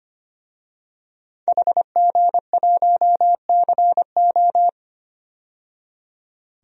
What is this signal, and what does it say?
Morse code, heard as a keyed tone, about 25 wpm, 700 Hz. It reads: HG1CO
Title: DXCC-100 Call Signs - 5 Characters - 25wpm